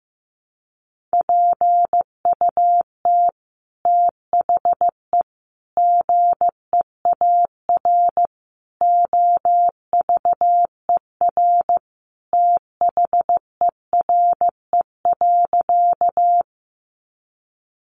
Morse code, heard as a keyed tone, about 15 wpm, 700 Hz. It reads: PUT THE GEAR OVER THERE.
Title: Sentences from Top 1400 Words 15wpm